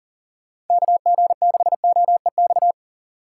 Morse code, heard as a keyed tone, about 50 wpm, 700 Hz. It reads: KG6OEX
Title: Speed Racing - USA 2x3 Call Signs 40wpm